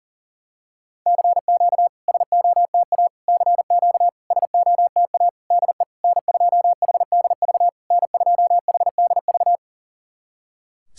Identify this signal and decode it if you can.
Morse code, heard as a keyed tone, about 42 wpm, 700 Hz. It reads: CQ SOTA CQ SOTA DE N2HDV N2HDV
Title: SOTA QSO Practice 40wpm